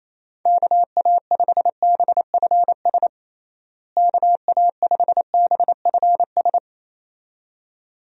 Morse code, heard as a keyed tone, about 28 wpm, 700 Hz. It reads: KA5BFS KA5BFS
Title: USA Call Signs - 6 Characters - Repeated 2x - 28wpm